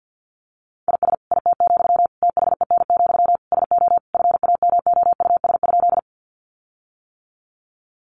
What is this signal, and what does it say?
Morse code, heard as a keyed tone, about 50 wpm, 700 Hz. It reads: IS IT, THEN, SO FAMOUS?
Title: Sentences from Top 1500 Words 50wpm